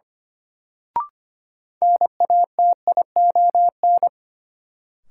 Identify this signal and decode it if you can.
Morse code, heard as a keyed tone, about 25 wpm, 700 Hz. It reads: NATION
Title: Sets of 3 Words - New Word Drill - 25wpm